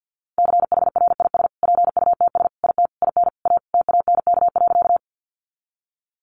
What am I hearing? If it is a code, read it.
Morse code, heard as a keyed tone, about 50 wpm, 700 Hz. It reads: CHRIS PUTS IT IN A TANK.
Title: Easy Phrases 50wpm